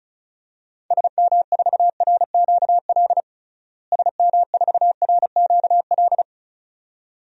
Morse code, heard as a keyed tone, about 36 wpm, 700 Hz. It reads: SM4RQL SM4RQL
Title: DXCC-100 Call Signs - 6 Characters - Repeated 2x - 35wpm